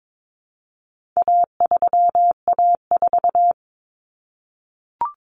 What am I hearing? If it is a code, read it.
Morse code, heard as a keyed tone, about 22 wpm, 700 Hz. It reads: A3A4
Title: Common Cut Numbers - 4-Digit 22wpm